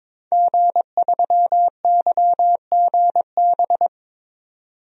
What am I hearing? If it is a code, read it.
Morse code, heard as a keyed tone, about 21 wpm, 700 Hz. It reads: G3YGB
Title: DXCC-100 Call Signs - 5 Characters - 22wpm